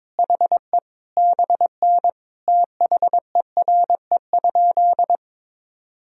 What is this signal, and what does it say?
Morse code, heard as a keyed tone, about 22 wpm, 700 Hz. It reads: HE BN THERE?
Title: Sentences from Top 100 Words and Abbreviations 22wpm